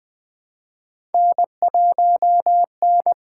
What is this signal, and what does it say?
Morse code, heard as a keyed tone, about 20 wpm, 700 Hz. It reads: N1N
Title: USA Call Signs - 5 Characters - 6x Character Spacing - 20wpm